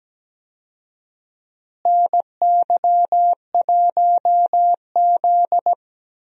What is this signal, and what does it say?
Morse code, heard as a keyed tone, about 17 wpm, 700 Hz. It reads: NY1Z
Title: USA Call Signs - 4 Characters - Repeated 3x - 17wpm